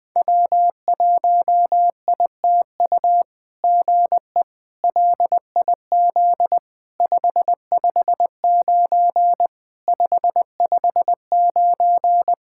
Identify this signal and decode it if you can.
Morse code, heard as a keyed tone, about 19 wpm, 700 Hz. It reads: W1ITU GE LIZ 559 559
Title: SOTA QSO Practice 20wpm